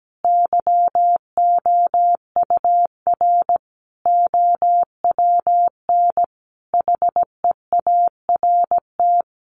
Morse code, heard as a keyed tone, about 17 wpm, 700 Hz. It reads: YOUR OWN HEART
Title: Sentences from Top 1200 Words 17wpm